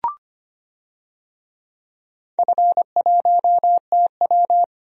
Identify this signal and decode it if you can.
Morse code, heard as a keyed tone, about 25 wpm, 700 Hz. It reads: F1TW
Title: DXCC-100 Call Signs - 4 Characters - 25wpm